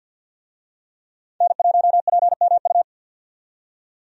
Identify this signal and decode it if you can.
Morse code, heard as a keyed tone, about 47 wpm, 700 Hz. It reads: N1PMU
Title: USA Call Signs - 5 Characters - 50wpm